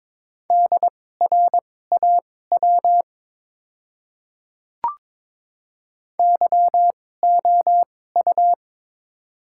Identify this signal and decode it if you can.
Morse code, heard as a keyed tone, about 21 wpm, spaced as for 16 wpm, 700 Hz. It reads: DRAW YOU
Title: 2x Character Spacing - Top 300 Words - 22wpm